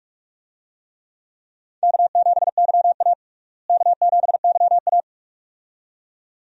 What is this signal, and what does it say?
Morse code, heard as a keyed tone, about 46 wpm, 700 Hz. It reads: K7YA K7YA
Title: USA Call Signs - 4 Characters - Repeated 2x - 45wpm